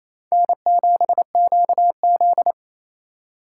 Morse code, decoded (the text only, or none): N7QZ